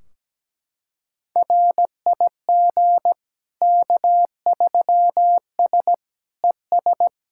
Morse code, heard as a keyed tone, about 17 wpm, 700 Hz. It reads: RIG K3S ES
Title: K1USN SST QSO Practice 17wpm